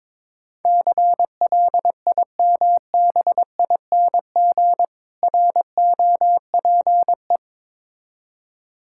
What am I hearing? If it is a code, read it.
Morse code, heard as a keyed tone, about 22 wpm, 690 Hz. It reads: CLIMBING ROPE
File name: Difficult Words with Hints - 22wpm